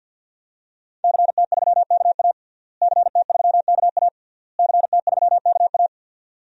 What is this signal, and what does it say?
Morse code, heard as a keyed tone, about 52 wpm, 700 Hz. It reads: CT3KA CT3KA CT3KA
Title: DXCC-100 Call Signs - 5 Characters - Repeated 3x - 50wpm